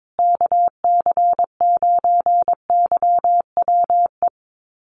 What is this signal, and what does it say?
Morse code, heard as a keyed tone, about 22 wpm, 700 Hz. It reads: KC9YWE
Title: USA Call Signs Repeated 3x 22wpm